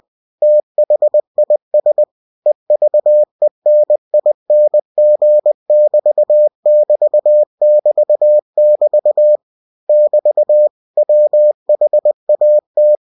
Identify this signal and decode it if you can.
Morse code, heard as a keyed tone, about 20 wpm, 600 Hz. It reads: THIS EVENING<BT><BT><BT><BT> <BT> WHAT